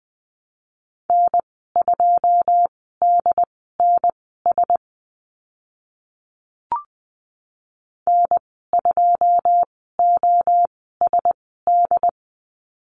Morse code, heard as a keyed tone, about 19 wpm, spaced as for 14 wpm, 700 Hz. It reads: N2DNS N2OSD
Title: 140 Lesson 23g - Letter f - Call Signs 20wpm